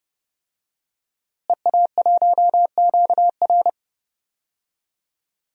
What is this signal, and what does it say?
Morse code, heard as a keyed tone, about 32 wpm, 700 Hz. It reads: EA1QR
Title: DXCC-100 Call Signs - 5 Characters - Repeated 2x - 30wpm